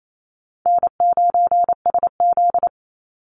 Morse code, heard as a keyed tone, about 28 wpm, 700 Hz. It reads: N9SZ
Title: Easy Call Signs Repeated 3x - 28wpm